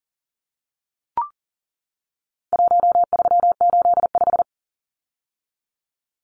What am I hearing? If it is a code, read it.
Morse code, heard as a keyed tone, about 40 wpm, 700 Hz. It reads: 1385
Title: Numbers - 4-Digit 40wpm